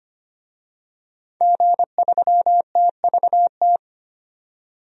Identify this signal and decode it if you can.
Morse code, heard as a keyed tone, about 25 wpm, 700 Hz. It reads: G3TVT
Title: DXCC-100 Call Signs - 5 Characters - 25wpm